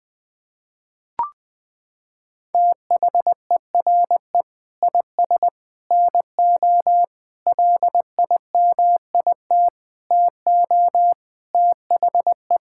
Morse code, heard as a keyed tone, about 20 wpm, 700 Hz. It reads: THERE IS NO LIMIT TO THE
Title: Sets of 8 Words - 20wpm